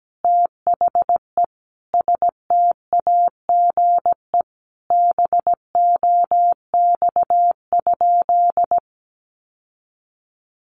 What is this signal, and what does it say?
Morse code, heard as a keyed tone, about 17 wpm, 700 Hz. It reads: THE STAGE BOX?
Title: Sentences from Top 1200 Words 17wpm